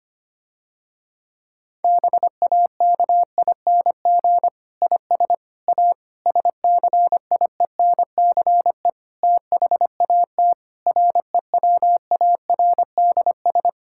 Morse code, heard as a keyed tone, about 25 wpm, 700 Hz. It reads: BAKING IS A SCIENCE THAT REWARDS
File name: practice / 25wpm / Difficult Sentences with Hints - 25wpm